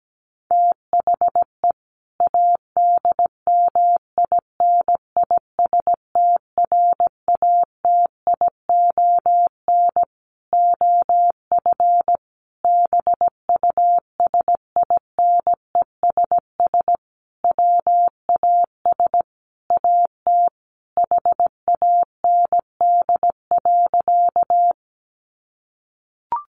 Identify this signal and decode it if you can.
Morse code, heard as a keyed tone, about 17 wpm, 700 Hz. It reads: THE ADMINISTRATION OF BUSINESS WAS AT HAND.
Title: Sentences from Top 1300 Words 17wpm